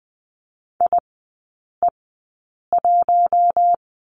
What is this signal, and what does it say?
Morse code, heard as a keyed tone, about 20 wpm, spaced as for 8 wpm, 700 Hz. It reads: IE1
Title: 041 Lesson 8e - Number 1 - Three Characters 20wpm